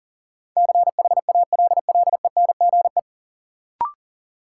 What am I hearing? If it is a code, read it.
Morse code, heard as a keyed tone, about 40 wpm, 700 Hz. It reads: CHALLENGE
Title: Sentences from Top 1300 Words - New Word Drill Alt - 40wpm